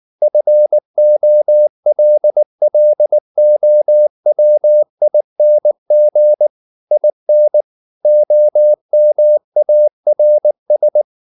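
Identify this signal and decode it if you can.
Morse code, heard as a keyed tone, about 19 wpm, 600 Hz. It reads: FOLLOWING IN OMARS
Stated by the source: Morse code at 19 wpm from CW med Gnister, reading Ken Follett's novel Never in English.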